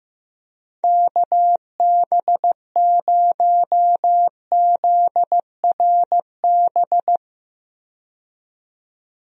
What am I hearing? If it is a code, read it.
Morse code, heard as a keyed tone, about 15 wpm, 700 Hz. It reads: KB0ZRB
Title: USA Call Signs - 6 Characters - 15wpm